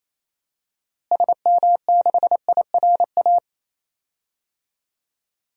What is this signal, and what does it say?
Morse code, heard as a keyed tone, about 28 wpm, 700 Hz. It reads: SM6IRA
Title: DXCC-100 Call Signs - 6 Characters - Repeated 3x - 28wpm